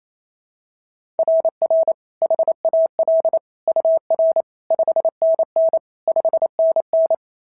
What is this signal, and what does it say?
Morse code, heard as a keyed tone, about 28 wpm, 690 Hz. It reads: RR HAL UR 5NN 5NN
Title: SOTA QSO Practice 28wpm